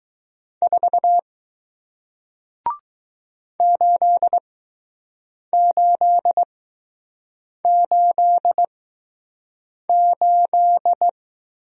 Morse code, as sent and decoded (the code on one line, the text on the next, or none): ....- / ---.. ---.. ---.. ---..
4 8888